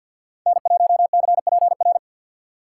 Morse code, heard as a keyed tone, about 64 wpm, 700 Hz. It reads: N1CPR